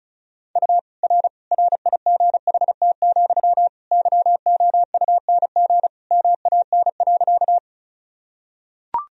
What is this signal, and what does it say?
Morse code, heard as a keyed tone, about 36 wpm, 700 Hz. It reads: U R RIGHT, YOUNG MAN.
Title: Sentences from Top 400 Words and Abbreviations 35wpm